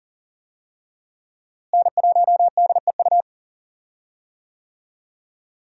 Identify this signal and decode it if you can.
Morse code, heard as a keyed tone, about 40 wpm, 700 Hz. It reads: N1DEU
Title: USA Call Signs - 5 Characters - Repeated 2x - 40wpm